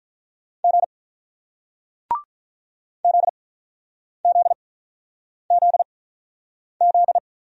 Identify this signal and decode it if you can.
Morse code, heard as a keyed tone, about 48 wpm, spaced as for 9 wpm, 700 Hz. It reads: G ZZZZ